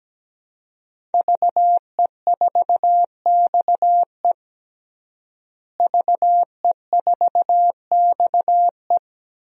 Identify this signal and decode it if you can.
Morse code, heard as a keyed tone, about 17 wpm, 700 Hz. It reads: VE4XE VE4XE
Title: DXCC-100 Call Signs - 5 Characters - Repeated 2x - 17wpm